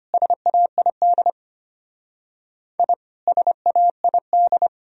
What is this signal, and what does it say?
Morse code, heard as a keyed tone, about 30 wpm, 700 Hz. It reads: SAID I SAID
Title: Speed-Racing - Sets of 2 Words 25wpm